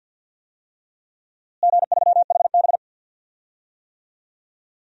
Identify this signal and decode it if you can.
Morse code, heard as a keyed tone, about 49 wpm, 700 Hz. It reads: G3HB
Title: DXCC-100 Call Signs - 4 Characters - Repeated 3x - 50wpm